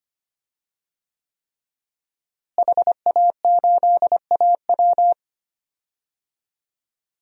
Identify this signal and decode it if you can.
Morse code, heard as a keyed tone, about 25 wpm, 700 Hz. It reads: HA8AW
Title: DXCC-100 Call Signs - 5 Characters - Repeated 3x - 25wpm